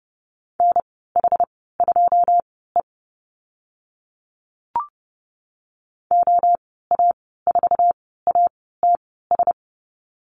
Morse code, heard as a keyed tone, about 30 wpm, spaced as for 17 wpm, 700 Hz. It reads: NH2E OA4ATS
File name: Lesson 14g - Number 2 - Call Signs 30wpm